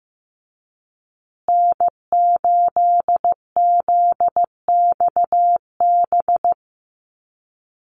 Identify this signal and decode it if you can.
Morse code, heard as a keyed tone, about 15 wpm, 700 Hz. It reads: N8ZXB
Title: USA Call Signs Repeated 3x 15wpm